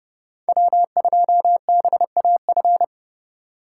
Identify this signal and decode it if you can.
Morse code, heard as a keyed tone, about 30 wpm, 700 Hz. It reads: W2BAF
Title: USA Call Signs - 2x Character Spacing - 30wpm